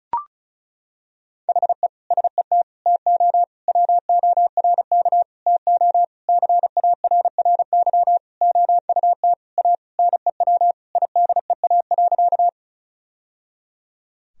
Morse code, heard as a keyed tone, about 34 wpm, 700 Hz. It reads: HE SET TO WORK TO CARRY OUT A NEW IDEA.
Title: Sentences from Top 1000 Words - Review Part 1 - 35wpm